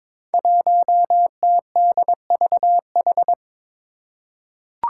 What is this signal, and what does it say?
Morse code, heard as a keyed tone, about 22 wpm, 700 Hz. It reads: 1TDVH
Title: Five Challenging Letter-Number Combinations 22wpm